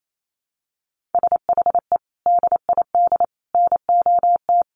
Morse code, heard as a keyed tone, about 28 wpm, 700 Hz. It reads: SHE DID NOT
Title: Sets of 5 Words Encore2 28wpm